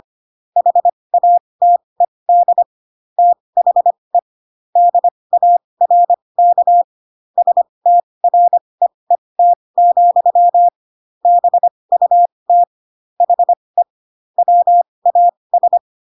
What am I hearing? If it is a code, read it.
Morse code, heard as a keyed tone, about 25 wpm, spaced as for 20 wpm, 700 Hz. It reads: HATED THE DARK STREET, BUT HE WAS
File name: Pinocchio / Pinocchio ch6 15 25 ik1hge